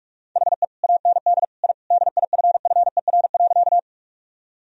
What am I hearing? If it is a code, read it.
Morse code, heard as a keyed tone, about 41 wpm, 700 Hz. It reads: HE AND I DIFFER.
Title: Sentences from Top 1000 Words - Review Part 1 - 45wpm